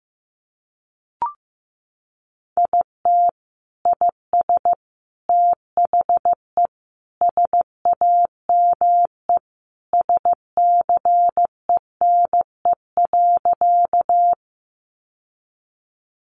Morse code, heard as a keyed tone, about 15 wpm, 700 Hz. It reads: IT IS THE SAME SCENE.
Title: Sentences from Top 1100 Words 15wpm